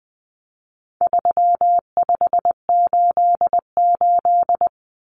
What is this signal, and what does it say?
Morse code, heard as a keyed tone, about 20 wpm, 700 Hz. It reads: 3588
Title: Common Cut Numbers - 4-Digit 20wpm